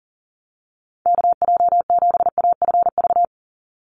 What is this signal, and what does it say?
Morse code, heard as a keyed tone, about 40 wpm, 690 Hz. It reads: KJ7AFV
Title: USA Call Signs 40wpm